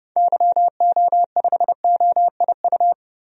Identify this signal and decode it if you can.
Morse code, heard as a keyed tone, about 30 wpm, 700 Hz. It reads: YO5OIU
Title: DXCC-100 Call Signs - 6 Characters - Repeated 3x - 30wpm